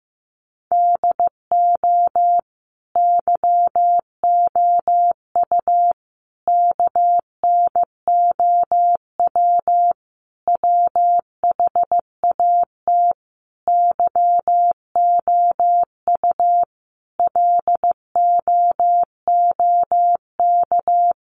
Morse code, heard as a keyed tone, about 15 wpm, 700 Hz. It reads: DO YOU KNOW WHAT YOU LOOK
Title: Sentences from Top 100 Words 15wpm